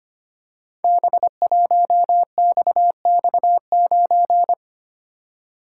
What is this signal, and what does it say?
Morse code, heard as a keyed tone, about 25 wpm, 700 Hz. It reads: B1XX9
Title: Code Groups - 5 Letter:Numbers - 25wpm